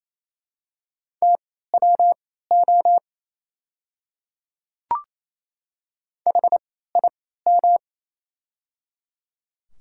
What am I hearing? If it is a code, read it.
Morse code, heard as a keyed tone, about 27 wpm, spaced as for 16 wpm, 700 Hz. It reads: TWO HIM
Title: 3x Character Spacing - Top 300 Words - 28wpm